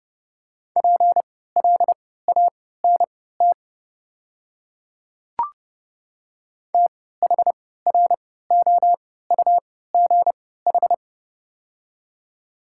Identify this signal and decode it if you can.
Morse code, heard as a keyed tone, about 30 wpm, spaced as for 17 wpm, 700 Hz. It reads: PLANT THROUGH
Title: 3x Character Spacing - Top 300 Words - 30wpm